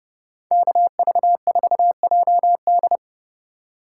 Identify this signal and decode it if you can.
Morse code, heard as a keyed tone, about 29 wpm, 700 Hz. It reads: KV4JD